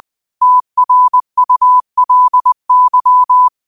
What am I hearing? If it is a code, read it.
Morse code, heard as a keyed tone, about 20 wpm, 1000 Hz. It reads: TRULY